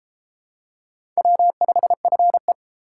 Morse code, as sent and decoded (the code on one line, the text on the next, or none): .-- ..... ..-. .
W5FE